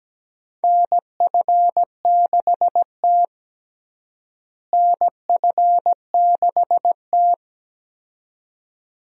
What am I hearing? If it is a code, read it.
Morse code, heard as a keyed tone, about 17 wpm, 700 Hz. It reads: NF6T NF6T
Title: USA Call Signs - 4 Characters - Repeated 2x - 17wpm